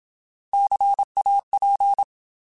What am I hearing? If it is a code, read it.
Morse code, heard as a keyed tone, about 27 wpm, 800 Hz. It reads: CAP
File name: Alphabetical